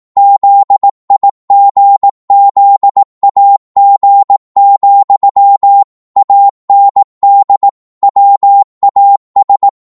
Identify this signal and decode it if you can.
Morse code, heard as a keyed tone, about 18 wpm, 800 Hz. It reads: ZIGZAG, AND WAS
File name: 18wpm